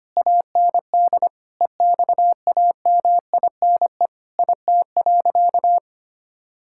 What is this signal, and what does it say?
Morse code, heard as a keyed tone, about 25 wpm, 700 Hz. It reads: AND EXAMINE IT.
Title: Sentences from Top 1600 Words 25wpm